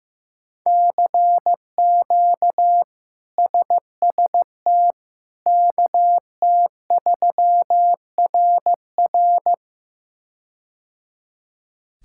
Morse code, heard as a keyed tone, about 16 wpm, 700 Hz. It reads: CQ SST KT3RR
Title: K1USN SST QSO Practice - 15wpm